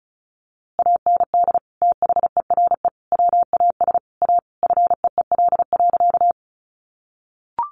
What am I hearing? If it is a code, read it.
Morse code, heard as a keyed tone, about 34 wpm, 700 Hz. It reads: AND THERE WAS A FEEL.
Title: A Sound Of Thunder 35wpm